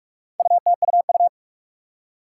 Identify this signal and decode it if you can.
Morse code, heard as a keyed tone, about 43 wpm, 700 Hz. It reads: UTUU
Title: Four Challenging Letter-Number Combinations 45wpm